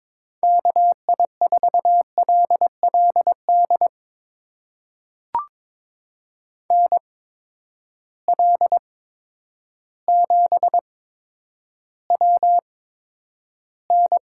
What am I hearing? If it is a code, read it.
Morse code, heard as a keyed tone, about 22 wpm, spaced as for 10 wpm, 700 Hz. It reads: KI4LLD NL7WN
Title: USA Call Signs - 8x Character Spacing - 22wpm